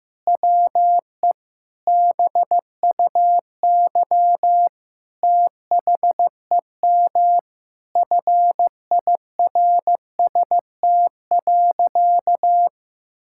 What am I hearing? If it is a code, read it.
Morse code, heard as a keyed tone, about 15 wpm, 700 Hz. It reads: WE BUY THEM FIRST.
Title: Sentences from Top 1000 Words - Review Part 6 - 15wpm